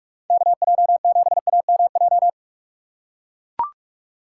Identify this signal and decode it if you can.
Morse code, heard as a keyed tone, about 42 wpm, 700 Hz. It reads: KJ7AMJ
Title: USA Call Signs - 6 Characters - 3x Character Spacing - 45wpm